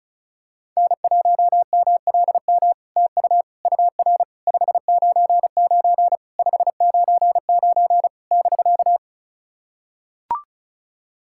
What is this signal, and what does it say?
Morse code, heard as a keyed tone, about 34 wpm, 700 Hz. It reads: N1MLM TU UR 599 599 <BK>
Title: SOTA QSO Practice 35wpm